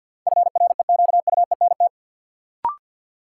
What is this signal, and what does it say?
Morse code, heard as a keyed tone, about 53 wpm, 700 Hz. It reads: FREQUENT
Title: Sentences from Top 1400 Words - New Word Drill - 50wpm